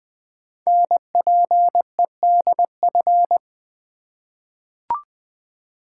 Morse code, heard as a keyed tone, about 20 wpm, 700 Hz. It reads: NPEDF
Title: Code Groups - 5 Letter:Numbers - 20wpm